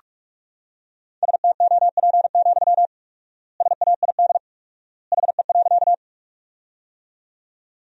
Morse code, heard as a keyed tone, about 47 wpm, 700 Hz. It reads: STOP, SAID HE.
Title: 4x Word-Spacing - Sentences from Top 300 Words - 45wpm